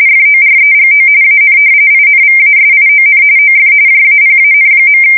RTTY, about 45.45 baud. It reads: FOREGO AN INVASION OF JAPAN.
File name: fsk170 / usnavy4